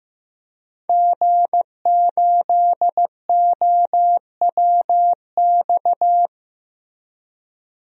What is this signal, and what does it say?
Morse code, heard as a keyed tone, about 15 wpm, 700 Hz. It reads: G8OWX
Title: DXCC-100 Call Signs - 5 Characters - Repeated 3x - 15wpm